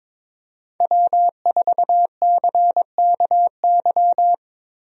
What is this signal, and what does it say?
Morse code, heard as a keyed tone, about 21 wpm, 700 Hz. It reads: W4CKY
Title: USA Call Signs - 5 Characters - Repeated 3x - 22wpm